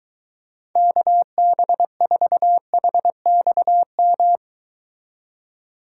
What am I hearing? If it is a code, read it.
Morse code, heard as a keyed tone, about 23 wpm, 700 Hz. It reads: KB4HXM